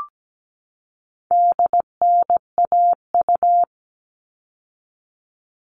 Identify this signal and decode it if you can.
Morse code, heard as a keyed tone, about 17 wpm, 700 Hz. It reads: DNAU
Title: All Cut Numbers - 4-Digit 17wpm